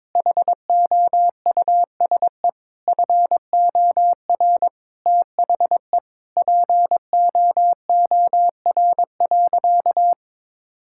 Morse code, heard as a keyed tone, about 22 wpm, 700 Hz. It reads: HOUSE FOR THE POOR.
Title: Sentences from Top 1600 Words 22wpm